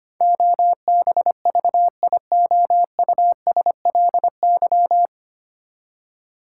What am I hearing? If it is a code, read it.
Morse code, heard as a keyed tone, about 25 wpm, 700 Hz. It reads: OBVIOUSLY
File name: Nine Letter Words - 25wpm